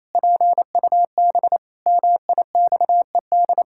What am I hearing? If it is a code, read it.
Morse code, heard as a keyed tone, about 28 wpm, 700 Hz. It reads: PUB MIXED
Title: Pangram Challenge v2 - 28wpm